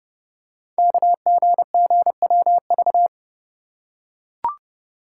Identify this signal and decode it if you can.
Morse code, heard as a keyed tone, about 28 wpm, 700 Hz. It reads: KGGWV
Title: Code Groups - 5 Letters - 30wpm